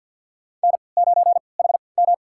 Morse code, heard as a keyed tone, about 47 wpm, spaced as for 28 wpm, 700 Hz. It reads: N9HM
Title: USA Call Signs - 4 Characters - 3x Character Spacing - 50wpm